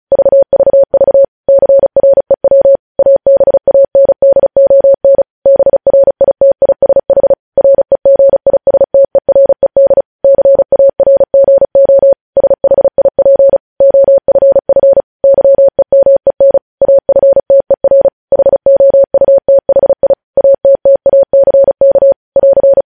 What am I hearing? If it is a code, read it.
Morse code, heard as a keyed tone, about 36 wpm, 550 Hz. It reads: VVV CREW ABANDON BRITISH REGISTERED CARGO SHIP OFF YEMEN AFTER HOUTHI ATTACK <AR>
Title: News Headlines at 35